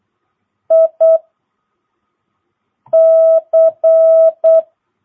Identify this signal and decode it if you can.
Morse code, heard as a keyed tone, about 8 wpm, 650 Hz. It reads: I C
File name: jh8cbh / 2022cw